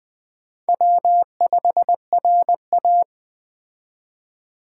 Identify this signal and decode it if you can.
Morse code, heard as a keyed tone, about 20 wpm, 700 Hz. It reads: W5RA
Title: Speed Racing - USA 1x2 Call Signs 17wpm